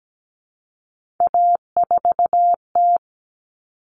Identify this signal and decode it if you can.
Morse code, heard as a keyed tone, about 17 wpm, 700 Hz. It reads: A4T
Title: All Cut Numbers - 3-Digit 17wpm